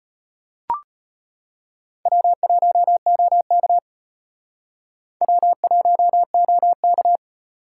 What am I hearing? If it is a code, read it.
Morse code, heard as a keyed tone, about 33 wpm, 700 Hz. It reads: W1OK W1OK